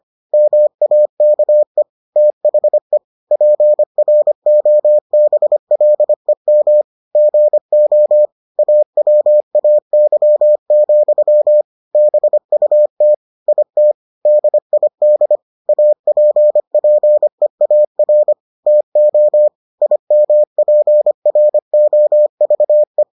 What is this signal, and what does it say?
Morse code, heard as a keyed tone, about 25 wpm, 600 Hz. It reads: MAKE THE PROBLEM GO AWAY, BUT IT DID APPEAR TO IMPROVE